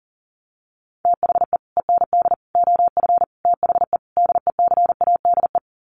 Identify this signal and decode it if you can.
Morse code, heard as a keyed tone, about 40 wpm, 700 Hz. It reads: THE END OF THE DECADE
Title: Sets of 5 Words Encore 40wpm